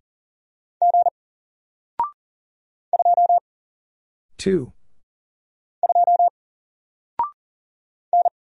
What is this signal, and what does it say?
Morse code, heard as a keyed tone, about 40 wpm, 700 Hz. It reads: G 2 2 N
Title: Single Letter-Number Alt 40wpm